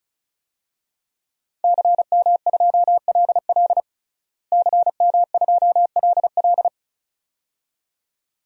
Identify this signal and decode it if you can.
Morse code, heard as a keyed tone, about 34 wpm, 700 Hz. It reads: CM2LL CM2LL
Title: DXCC-100 Call Signs - 5 Characters - Repeated 2x - 35wpm